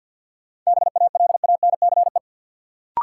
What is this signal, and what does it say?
Morse code, heard as a keyed tone, about 53 wpm, 700 Hz. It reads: BALANCE
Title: Sentences from Top 900 Words - New Word Drill - 50wpm